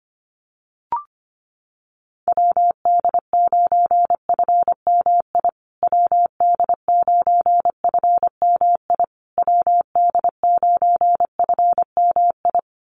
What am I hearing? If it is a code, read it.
Morse code, heard as a keyed tone, about 25 wpm, 700 Hz. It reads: WD9FMI WD9FMI WD9FMI
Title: USA Call Signs Repeated 3x 25wpm